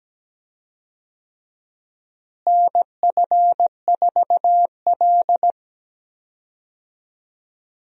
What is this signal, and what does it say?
Morse code, heard as a keyed tone, about 17 wpm, 700 Hz. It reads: NF4L
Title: USA Call Signs - 4 Characters - Repeated 3x - 17wpm